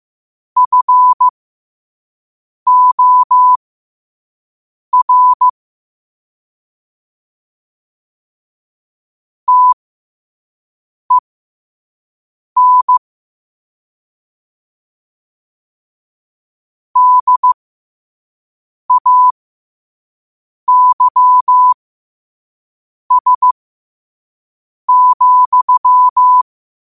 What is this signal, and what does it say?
Morse code, heard as a keyed tone, about 15 wpm, spaced as for 5 wpm, 1000 Hz. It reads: FOR TEN DAYS,
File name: 5wpm